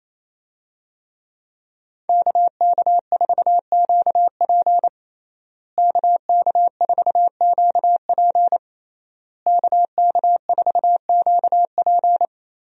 Morse code, heard as a keyed tone, about 28 wpm, 700 Hz. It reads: KK4QP KK4QP KK4QP
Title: USA Call Signs - 5 Characters - Repeated 3x - 28wpm